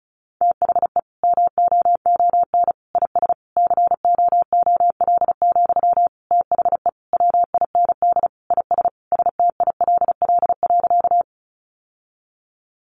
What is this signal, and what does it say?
Morse code, heard as a keyed tone, about 34 wpm, 700 Hz. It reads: THE MOON IS COOL, THE WIND IS STILL.
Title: Sentences from Top 1200 Words 35wpm